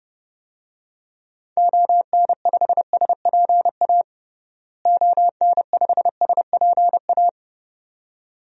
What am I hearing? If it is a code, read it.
Morse code, heard as a keyed tone, about 32 wpm, 700 Hz. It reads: ON5SPA ON5SPA
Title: DXCC-100 Call Signs - 6 Characters - Repeated 2x - 30wpm